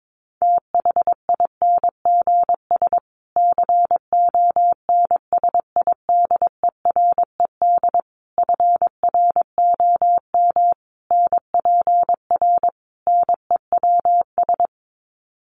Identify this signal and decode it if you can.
Morse code, heard as a keyed tone, about 22 wpm, 700 Hz. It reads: THINGS CONSIDERED FROM NPR NEWS
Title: Sets of 5 Words Encore 22wpm